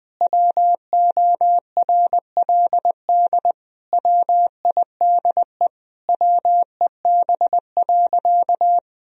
Morse code, heard as a keyed tone, about 20 wpm, 700 Hz. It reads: WORLD WIDE WEB.
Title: Sentences from Top 1000 Words - Review Part 4 - 20wpm